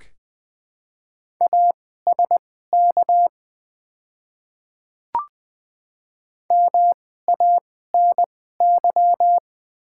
Morse code, heard as a keyed tone, about 20 wpm, spaced as for 14 wpm, 700 Hz. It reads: ASK MANY
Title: 2x Character Spacing - Top 200 Words - 20wpm